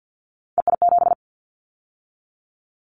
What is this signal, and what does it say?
Morse code, heard as a keyed tone, about 50 wpm, 700 Hz. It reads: EI7
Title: Callsign Prefixes for Top 50 DXCC - 50wpm